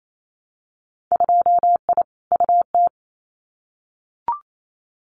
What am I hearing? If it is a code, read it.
Morse code, heard as a keyed tone, about 28 wpm, 700 Hz. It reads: 2I UT
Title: Winter Field Day Exchanges 28wpm